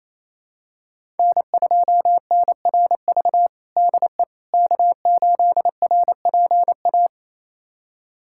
Morse code, heard as a keyed tone, about 27 wpm, 700 Hz. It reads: N2NRV DE K8RPA
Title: USA Callsign DE Callsign - 5 Characters - 28wpm